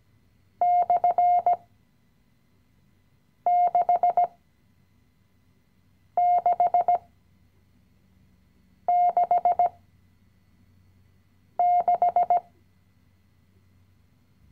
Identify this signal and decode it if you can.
Morse code, heard as a keyed tone, about 16 wpm, spaced as for 4 wpm, 710 Hz. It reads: /6666
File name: CW-Online-Kurs / Anfänger / Lektion 016